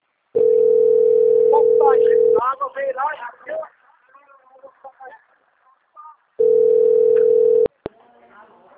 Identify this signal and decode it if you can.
Ringback tone (North America), 440 + 480 Hz, with speech.
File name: out- - - - - .1699